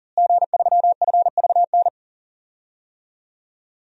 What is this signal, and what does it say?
Morse code, heard as a keyed tone, about 40 wpm, 700 Hz. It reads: G3FVN
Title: DXCC-100 Call Signs - 5 Characters - Repeated 3x - 40wpm